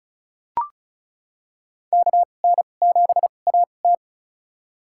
Morse code, heard as a keyed tone, about 34 wpm, spaced as for 25 wpm, 700 Hz. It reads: KN7AT
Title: USA Call Signs - 5 Characters - 2x Character Spacing - 35wpm